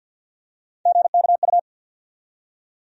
Morse code, heard as a keyed tone, about 52 wpm, 700 Hz. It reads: GKU
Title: USA Call Sign Postfixes - 3 Letters 50wpm